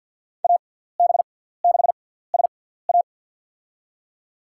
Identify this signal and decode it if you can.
Morse code, heard as a keyed tone, about 53 wpm, spaced as for 17 wpm, 700 Hz. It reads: AB6SA
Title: USA Call Signs - 6x Character Spacing - 50wpm